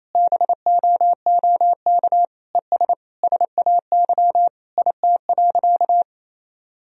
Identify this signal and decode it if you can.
Morse code, heard as a keyed tone, about 28 wpm, 700 Hz. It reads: BOOK ES SAY IT.
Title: Sentences from Top 400 Words and Abbreviations 28wpm